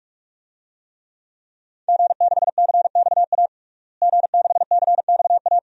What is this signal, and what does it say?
Morse code, heard as a keyed tone, about 46 wpm, 700 Hz. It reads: G6CXA G6CXA
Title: DXCC-100 Call Signs - 5 Characters - Repeated 2x - 45wpm